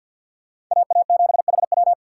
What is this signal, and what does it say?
Morse code, heard as a keyed tone, about 46 wpm, 700 Hz. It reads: AA7HW
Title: USA Call Signs - 5 Characters - 2x Character Spacing - 50wpm